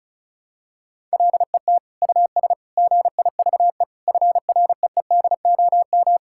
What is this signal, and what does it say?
Morse code, heard as a keyed tone, about 36 wpm, 700 Hz. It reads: LET US GIVE FREEDOM
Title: Sentences from Top 1000 Words - Review Part 5 - 35wpm